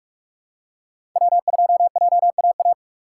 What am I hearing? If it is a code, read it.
Morse code, heard as a keyed tone, about 42 wpm, 700 Hz. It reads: W2JAA
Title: USA Call Signs - 5 Characters - 4x Character Spacing - 45wpm